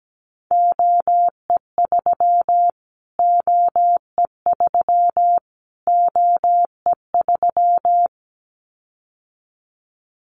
Morse code, heard as a keyed tone, about 17 wpm, 700 Hz. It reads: OE3 OE3 OE3
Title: Callsign Prefixes for Top 25 DXCC - Repeated 3x - 17wpm